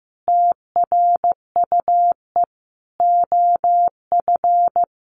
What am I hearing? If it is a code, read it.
Morse code, heard as a keyed tone, about 15 wpm, 700 Hz. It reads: TRUE OF
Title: Sentences from Top 1400 Words 15wpm